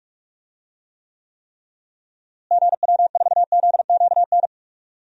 Morse code, heard as a keyed tone, about 45 wpm, 700 Hz. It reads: GW4ZQN
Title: DXCC-100 Call Signs - 6 Characters - Repeated 3x - 45wpm